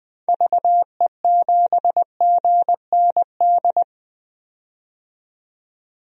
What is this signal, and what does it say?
Morse code, heard as a keyed tone, about 20 wpm, 700 Hz. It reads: VE7GND
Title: DXCC-100 Call Signs - 6 Characters - 20wpm